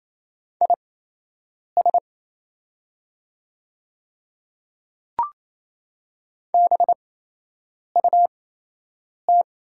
Morse code, heard as a keyed tone, about 28 wpm, spaced as for 8 wpm, 700 Hz. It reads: IS BUT
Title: 8x Character Spacing - Top 100 Words - 28wpm